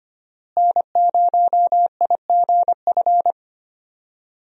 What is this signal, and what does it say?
Morse code, heard as a keyed tone, about 25 wpm, 700 Hz. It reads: N0IGF
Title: USA Call Signs - 5 Characters - Repeated 3x - 25wpm